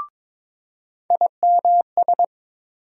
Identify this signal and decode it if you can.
Morse code, heard as a keyed tone, about 22 wpm, 700 Hz. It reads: IMS
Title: USA Call Sign Postfixes - 3 Letters 22wpm